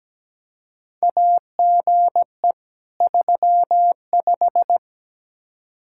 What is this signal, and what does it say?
Morse code, heard as a keyed tone, about 17 wpm, 700 Hz. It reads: AGE 35
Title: Practice QSO - Exchange 3 - More Info - 17wpm